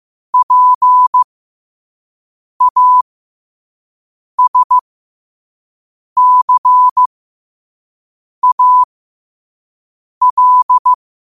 Morse code, heard as a keyed tone, about 15 wpm, spaced as for 5 wpm, 1000 Hz. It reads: PASCAL
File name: qotdfiles / 5wpm